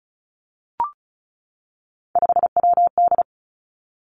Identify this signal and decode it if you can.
Morse code, heard as a keyed tone, about 36 wpm, 700 Hz. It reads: 5WD
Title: Three Challenging Letter-Number Combinations 35wpm